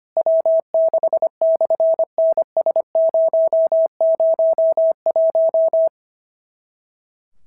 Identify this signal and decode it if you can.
Morse code, heard as a keyed tone, about 25 wpm, 650 Hz. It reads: W6/NS001
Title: SOTA QSO Practice 25wpm